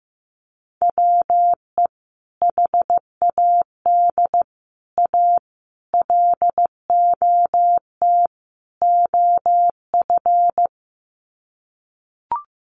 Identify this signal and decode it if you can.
Morse code, heard as a keyed tone, about 15 wpm, 700 Hz. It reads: WE HAD A LOT OF
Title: Sets of 5 Words Encore 15wpm